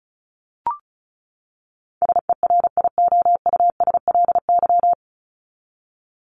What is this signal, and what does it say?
Morse code, heard as a keyed tone, about 34 wpm, 700 Hz. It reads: SERIOUSLY
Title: Top 1001-1250 Words 35wpm